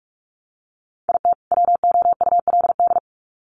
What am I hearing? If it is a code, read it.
Morse code, heard as a keyed tone, about 44 wpm, 700 Hz. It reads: IT WOULD
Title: Sets of 2 Words 45wpm